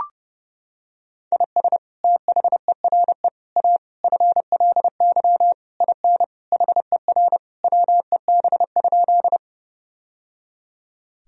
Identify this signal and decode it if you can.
Morse code, heard as a keyed tone, about 30 wpm, 700 Hz. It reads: IS THERE A FLY IN HER WEB?
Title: Sentences from Top 1000 Words - Review Part 4 - 30wpm